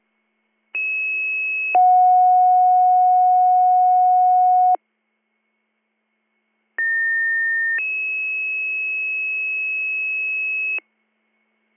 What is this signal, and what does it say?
Two-tone page, heard as a one-second tone, then a three-second tone, about 2600 then 730 Hz, about 1800 then 2600 Hz. It is heard from 0.7 to 10.8 s.